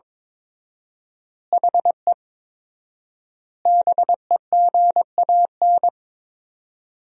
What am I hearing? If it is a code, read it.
Morse code, heard as a keyed tone, about 22 wpm, 700 Hz. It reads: HE BEGAN
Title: 4x Word-Spacing - Sentences from Top 300 Words - 22wpm